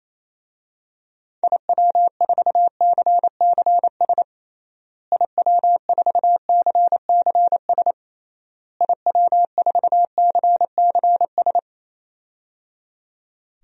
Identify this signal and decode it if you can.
Morse code, heard as a keyed tone, about 28 wpm, 700 Hz. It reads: IW4CCS IW4CCS IW4CCS
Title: DXCC-100 Call Signs - 6 Characters - Repeated 3x - 28wpm